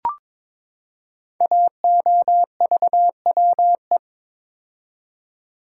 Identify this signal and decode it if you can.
Morse code, heard as a keyed tone, about 22 wpm, 700 Hz. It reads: AOVWE
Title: Code Groups - 5 Letters - 22wpm